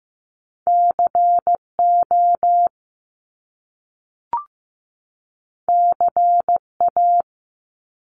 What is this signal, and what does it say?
Morse code, heard as a keyed tone, about 15 wpm, 700 Hz. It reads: CO CA
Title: US States 15wpm